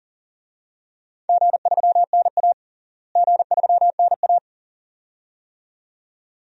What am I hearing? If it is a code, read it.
Morse code, heard as a keyed tone, about 37 wpm, 700 Hz. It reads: G3NA G3NA
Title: DXCC-100 Call Signs - 4 Characters - Repeated 2x - 40wpm